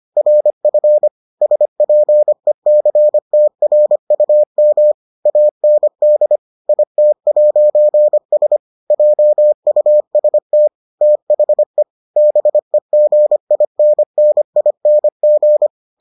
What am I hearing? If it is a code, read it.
Morse code, heard as a keyed tone, about 25 wpm, 600 Hz. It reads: RF SPECTRUM AND IT'S JUST THE BEGINNING